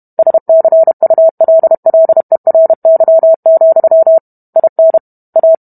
Morse code, heard as a keyed tone, about 32 wpm, 650 Hz. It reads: SCULLERY, IN A